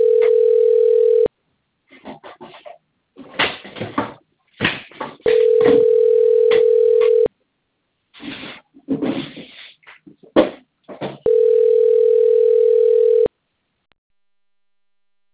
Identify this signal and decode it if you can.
Ringback tone (North America), 440 + 480 Hz.
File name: rg-1600-unknown- - - .46867